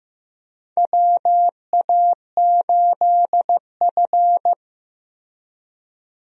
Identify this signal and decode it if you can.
Morse code, heard as a keyed tone, about 15 wpm, 700 Hz. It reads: WA8F
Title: USA Call Signs - 4 Characters - Repeated 3x - 15wpm